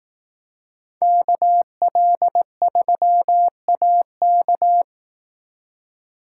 Morse code, heard as a keyed tone, about 18 wpm, 700 Hz. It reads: KL3AK
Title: Speed Racing - USA 2x2 Call Signs 15wpm